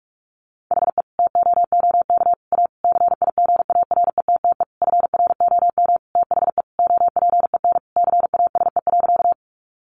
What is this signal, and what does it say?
Morse code, heard as a keyed tone, about 44 wpm, 700 Hz. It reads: HE TOOK A CIGARETTE FROM THE OPEN CASE.
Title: Sentences from Top 1000 Words 45wpm